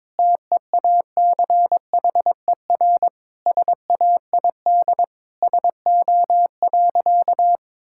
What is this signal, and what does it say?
Morse code, heard as a keyed tone, about 22 wpm, 700 Hz. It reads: TEACHER SAID SO.
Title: Sentences from Top 1000 Words - Review Part 6 - 22wpm